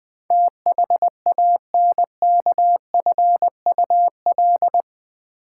Morse code, heard as a keyed tone, about 20 wpm, 700 Hz. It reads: THANKFUL
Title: Build A Word - 4 Letters At A Time - 10-Letter Words - 20wpm